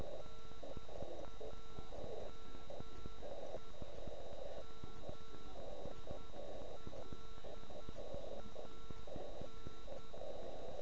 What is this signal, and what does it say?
Morse code, heard as a keyed tone, about 9 wpm, 1340 Hz. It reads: NAME MILAN